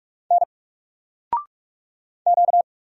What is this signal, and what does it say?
Morse code, heard as a keyed tone, about 44 wpm, 700 Hz. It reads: N Q
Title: Single Character (Letter:Number:Punctuation:Prosign) 45wpm